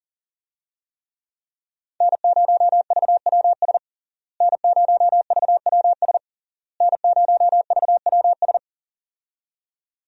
Morse code, heard as a keyed tone, about 37 wpm, 700 Hz. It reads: N0VWS N0VWS N0VWS
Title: USA Call Signs - 5 Characters - Repeated 3x - 40wpm